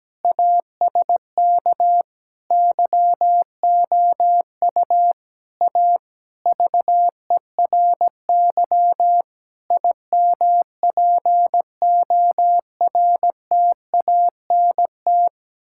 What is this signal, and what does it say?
Morse code, heard as a keyed tone, about 17 wpm, 700 Hz. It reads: ASK YOU A VERY IMPORTANT
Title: Sentences from Top 1000 Words - Review Part 6 - 17wpm